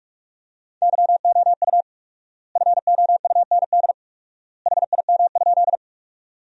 Morse code, heard as a keyed tone, about 46 wpm, 700 Hz. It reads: YOU FOUND HIM?
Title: 4x Word-Spacing - Sentences from Top 300 Words - 45wpm